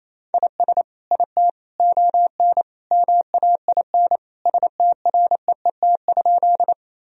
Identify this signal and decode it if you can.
Morse code, heard as a keyed tone, about 28 wpm, 700 Hz. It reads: IS IT ON MAIN STREET?
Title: Sentences from Top 1000 Words - Review Part 2 - 28wpm